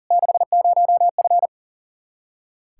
Morse code, heard as a keyed tone, about 40 wpm, 700 Hz. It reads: 60F
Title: Practice QSO - Exchange 2 - Info - 40wpm